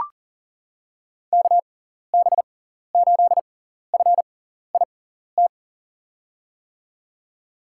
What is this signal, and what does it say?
Morse code, heard as a keyed tone, about 40 wpm, spaced as for 14 wpm, 700 Hz. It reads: KB8FIT
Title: USA Call Signs - 6 Characters - 6x Character Spacing - 40wpm